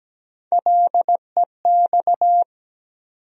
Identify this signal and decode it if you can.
Morse code, heard as a keyed tone, about 17 wpm, 700 Hz. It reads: LEX
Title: USA Call Sign Postfixes - 3 Letters 17wpm